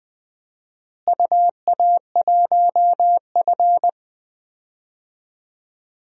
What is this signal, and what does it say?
Morse code, heard as a keyed tone, about 21 wpm, 700 Hz. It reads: UA1F
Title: DXCC-100 Call Signs - 4 Characters - Repeated 3x - 20wpm